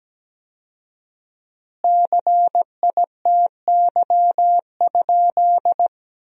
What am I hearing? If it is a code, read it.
Morse code, heard as a keyed tone, about 17 wpm, 700 Hz. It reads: CITY?
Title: 4x Word-Spacing - Sentences from Top 300 Words - 17wpm